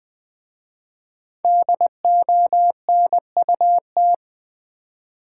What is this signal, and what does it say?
Morse code, heard as a keyed tone, about 20 wpm, 690 Hz. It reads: DONUT
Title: Difficult Words with Hints - 20wpm